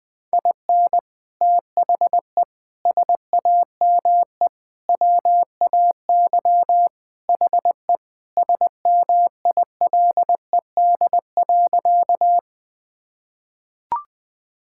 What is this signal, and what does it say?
Morse code, heard as a keyed tone, about 20 wpm, 700 Hz. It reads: IN THE SAME WAY HE SMILED.
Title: Sentences with 4-Word Phrases - Spoken Beginning - 20wpm